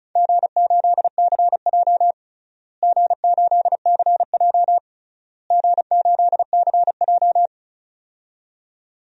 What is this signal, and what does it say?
Morse code, heard as a keyed tone, about 34 wpm, 700 Hz. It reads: G8CJ G8CJ G8CJ
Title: DXCC-100 Call Signs - 4 Characters - Repeated 3x - 35wpm